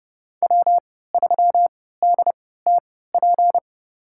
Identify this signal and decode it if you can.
Morse code, heard as a keyed tone, about 30 wpm, spaced as for 17 wpm, 700 Hz. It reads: W3DTP
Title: USA Call Signs - 5 Characters - 3x Character Spacing - 30wpm